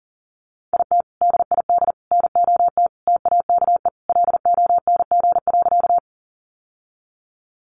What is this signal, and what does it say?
Morse code, heard as a keyed tone, about 40 wpm, 700 Hz. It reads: IT DID NOT TAKE LONG.
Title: Sentences from Top 600 Words 40wpm